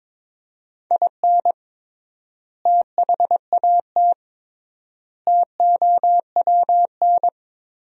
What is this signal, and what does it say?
Morse code, heard as a keyed tone, about 22 wpm, 700 Hz. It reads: IN THAT TOWN
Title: 3x Word-Spacing - Sentences from Top 500 Words - 22wpm